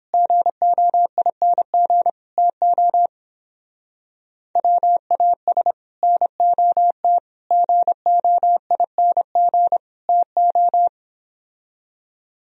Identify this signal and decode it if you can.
Morse code, heard as a keyed tone, about 29 wpm, 700 Hz. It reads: GOING TO WAS NOT GOING TO